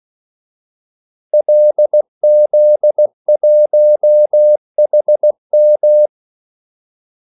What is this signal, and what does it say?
Morse code, heard as a keyed tone, about 16 wpm, 600 Hz. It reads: LZ1HM
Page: Kaldesignaler 16 wpm